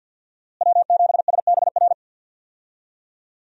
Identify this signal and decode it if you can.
Morse code, heard as a keyed tone, about 46 wpm, 700 Hz. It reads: W7SBR
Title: USA Call Signs - 5 Characters - Repeated 3x - 50wpm